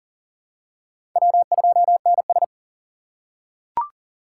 Morse code, heard as a keyed tone, about 40 wpm, 700 Hz. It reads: W2NS
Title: USA Call Signs - 4 Characters - 4x Character Spacing - 40wpm